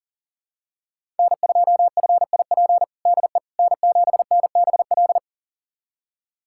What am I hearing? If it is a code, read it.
Morse code, heard as a keyed tone, about 37 wpm, 700 Hz. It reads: N2FIP DE N7NBL
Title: USA Callsign DE Callsign - 5 Characters - 40wpm